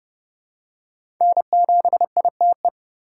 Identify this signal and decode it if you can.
Morse code, heard as a keyed tone, about 30 wpm, 700 Hz. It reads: N7ITE
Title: USA Call Signs - 5 Characters - 4x Character Spacing - 30wpm